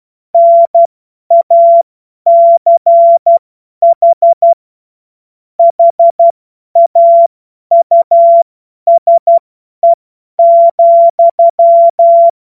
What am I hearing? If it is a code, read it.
Morse code, heard as a keyed tone, about 12 wpm, spaced as for 10 wpm, 680 Hz. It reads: NACH HAUSE,
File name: Morsekurs / Witze / Witz 07. Juni 2022 50 F70